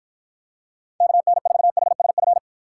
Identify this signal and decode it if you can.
Morse code, heard as a keyed tone, about 53 wpm, 700 Hz. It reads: KN4HSF